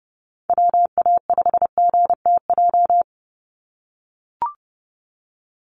Morse code, heard as a keyed tone, about 29 wpm, 700 Hz. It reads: WA5GTJ
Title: USA Call Signs 30wpm